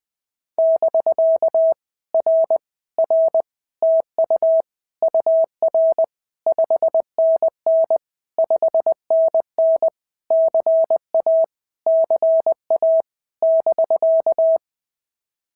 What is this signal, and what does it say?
Morse code, heard as a keyed tone, about 23 wpm, 650 Hz. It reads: <BK> R R TU UR 5NN 5NN CA CA <BK>
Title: POTA QSO Practice - 20wpm